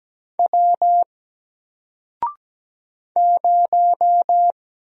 Morse code, heard as a keyed tone, about 17 wpm, 700 Hz. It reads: W 0
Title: Single Letter-Number Alt 17wpm